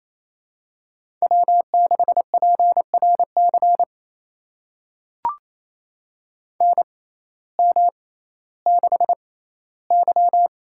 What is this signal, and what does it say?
Morse code, heard as a keyed tone, about 28 wpm, spaced as for 15 wpm, 700 Hz. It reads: W6PRC NM6Y
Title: USA Call Signs - 6x Character Spacing - 28wpm